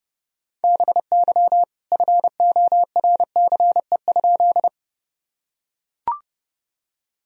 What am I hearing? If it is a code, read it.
Morse code, heard as a keyed tone, about 30 wpm, 700 Hz. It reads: BY FORCE?
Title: Sentences from Top 1000 Words - Review Part 3 - 30wpm